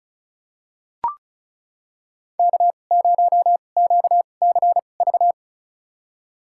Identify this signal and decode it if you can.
Morse code, heard as a keyed tone, about 36 wpm, spaced as for 25 wpm, 700 Hz. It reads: K0QCV
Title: USA Call Signs - 2x Character Spacing - 35wpm